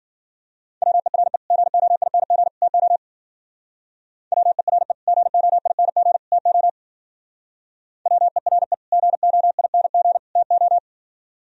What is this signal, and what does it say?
Morse code, heard as a keyed tone, about 51 wpm, 700 Hz. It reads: WERE GOING TO WERE GOING TO WERE GOING TO